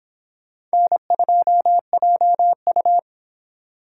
Morse code, heard as a keyed tone, about 25 wpm, 700 Hz. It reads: N2JU